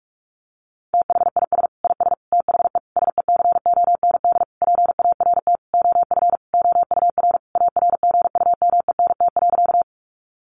Morse code, heard as a keyed tone, about 44 wpm, 700 Hz. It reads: THIS IS THE SECOND PART OF OUR ARGUMENT.
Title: Sentences from Top 1400 Words 45wpm